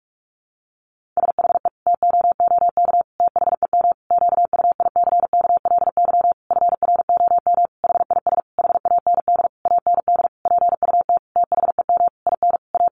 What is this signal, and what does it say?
Morse code, heard as a keyed tone, about 44 wpm, 700 Hz. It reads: SHE TOOK THEM QUICKLY FROM HIS HAND AND PUT THEM IN A
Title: Sentences from Top 1300 Words 45wpm